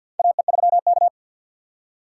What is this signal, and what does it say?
Morse code, heard as a keyed tone, about 53 wpm, 700 Hz. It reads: AE3K
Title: USA Call Signs - 4 Characters - 3x Character Spacing - 50wpm